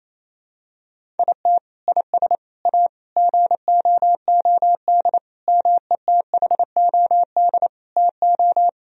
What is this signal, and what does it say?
Morse code, heard as a keyed tone, about 28 wpm, 700 Hz. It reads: IT IS A GOOD METHOD TO
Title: Sentences from Top 1000 Words - Review Part 3 - 28wpm